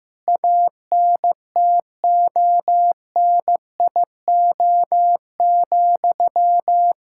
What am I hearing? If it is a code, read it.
Morse code, heard as a keyed tone, about 15 wpm, 700 Hz. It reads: ANTONIO,
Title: K1USN SST QSO Practice 15wpm